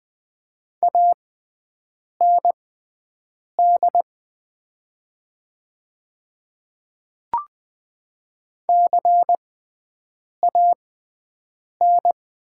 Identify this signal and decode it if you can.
Morse code, heard as a keyed tone, about 20 wpm, spaced as for 7 wpm, 700 Hz. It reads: AND CAN
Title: 6x Character Spacing - Top 200 Words - 20wpm